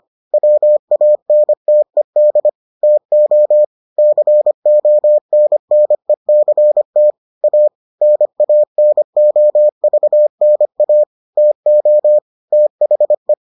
Morse code, heard as a keyed tone, about 25 wpm, 600 Hz. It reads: WANTED TO CONNECT A NANOVNA TO THE